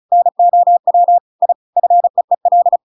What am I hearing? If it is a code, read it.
Morse code, heard as a keyed tone, about 34 wpm, 700 Hz. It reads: NOW I FEEL
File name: Pinocchio / Pinocchio ch7 35 35 ik1hge